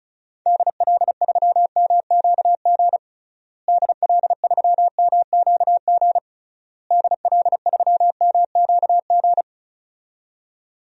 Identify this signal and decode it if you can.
Morse code, heard as a keyed tone, about 36 wpm, 700 Hz. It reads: DL3MQG DL3MQG DL3MQG
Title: DXCC-100 Call Signs - 6 Characters - Repeated 3x - 35wpm